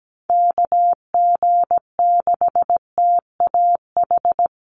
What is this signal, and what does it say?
Morse code, heard as a keyed tone, about 17 wpm, 700 Hz. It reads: KG6TAH
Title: USA Call Signs Repeated 3x 17wpm